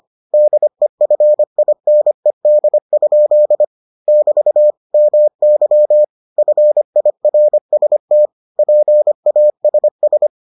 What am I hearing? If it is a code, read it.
Morse code, heard as a keyed tone, about 25 wpm, 600 Hz. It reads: DEFINED? <BT> MY FIRST PASS